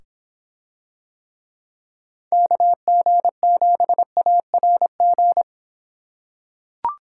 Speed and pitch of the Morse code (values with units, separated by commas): 24 wpm, 700 Hz